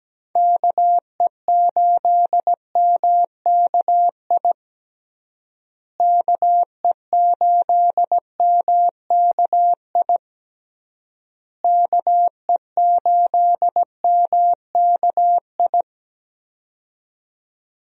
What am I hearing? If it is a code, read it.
Morse code, heard as a keyed tone, about 17 wpm, 700 Hz. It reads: KE8MKI KE8MKI KE8MKI
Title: USA Call Signs - 6 Characters - Repeated 3x - 17wpm